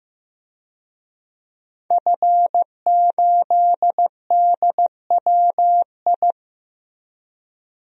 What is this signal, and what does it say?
Morse code, heard as a keyed tone, about 15 wpm, 700 Hz. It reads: F8DWI
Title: DXCC-100 Call Signs - 5 Characters - Repeated 3x - 15wpm